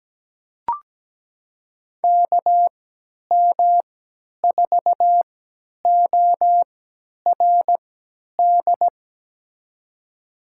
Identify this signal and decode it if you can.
Morse code, heard as a keyed tone, about 17 wpm, spaced as for 10 wpm, 700 Hz. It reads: KM4ORD
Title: USA Call Signs - 3x Character Spacing - 17wpm